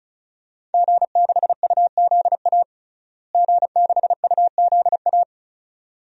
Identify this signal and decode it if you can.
Morse code, heard as a keyed tone, about 34 wpm, 700 Hz. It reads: G6UZA G6UZA
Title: DXCC-100 Call Signs - 5 Characters - Repeated 2x - 35wpm